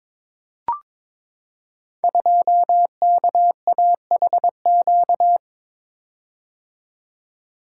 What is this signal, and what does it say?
Morse code, heard as a keyed tone, about 22 wpm, 700 Hz. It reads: 2KAHQ
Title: Code Groups - 5 Letter:Numbers - 22wpm